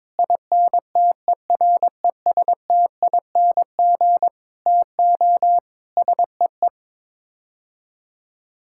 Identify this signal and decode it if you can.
Morse code, heard as a keyed tone, about 22 wpm, 700 Hz. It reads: INTERESTING TO SEE
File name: practice / 22wpm / Sets of 7 Words - 22wpm